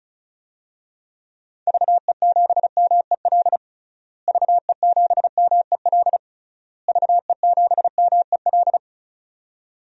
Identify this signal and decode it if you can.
Morse code, heard as a keyed tone, about 34 wpm, 700 Hz. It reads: VE7MEL VE7MEL VE7MEL
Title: DXCC-100 Call Signs - 6 Characters - Repeated 3x - 35wpm